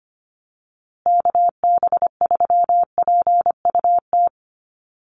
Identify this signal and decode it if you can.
Morse code, heard as a keyed tone, about 24 wpm, 700 Hz. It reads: KB3PUT
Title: USA Call Signs Repeated 3x 25wpm